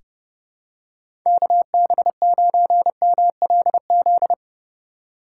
Morse code, heard as a keyed tone, about 31 wpm, 700 Hz. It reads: KB9MLZ
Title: USA Call Signs - 6 Characters - 8x Character Spacing - 30wpm